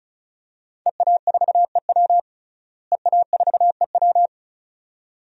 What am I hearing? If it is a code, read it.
Morse code, heard as a keyed tone, about 36 wpm, 700 Hz. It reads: EA4EW EA4EW
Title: DXCC-100 Call Signs - 5 Characters - Repeated 2x - 35wpm